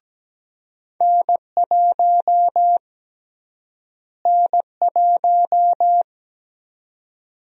Morse code, heard as a keyed tone, about 17 wpm, 700 Hz. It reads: N1 N1
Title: USA Call Sign Prefixes - Repeated 2x - 17wpm